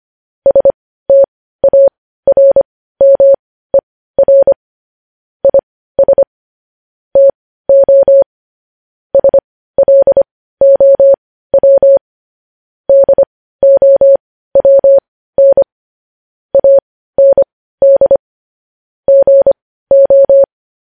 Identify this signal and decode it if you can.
Morse code, heard as a keyed tone, about 25 wpm, spaced as for 15 wpm, 550 Hz. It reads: STARMER IS TO SLOW DOWN AND GO
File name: podcast / News headlines-1x-25x15